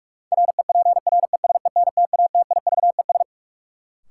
Morse code, heard as a keyed tone, about 43 wpm, 700 Hz. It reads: REPRESENTATIVES
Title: Sets of 5 Words Encore - New Word Drill - 45wpm